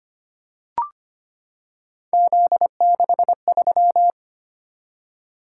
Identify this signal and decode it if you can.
Morse code, heard as a keyed tone, about 25 wpm, 700 Hz. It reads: Z63
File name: Three Letter-Number Combinations - 25wpm